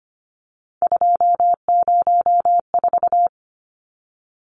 Morse code, heard as a keyed tone, about 25 wpm, 700 Hz. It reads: 204
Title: Numbers - 3-Digit 25wpm